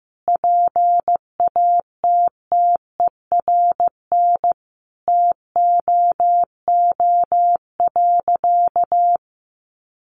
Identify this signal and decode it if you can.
Morse code, heard as a keyed tone, about 15 wpm, 700 Hz. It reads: PATTERN TOO.
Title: Sentences from Top 1000 Words 15wpm